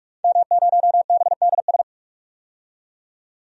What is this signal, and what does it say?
Morse code, heard as a keyed tone, about 46 wpm, 700 Hz. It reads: M0BDS
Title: DXCC-100 Call Signs - 5 Characters - 45wpm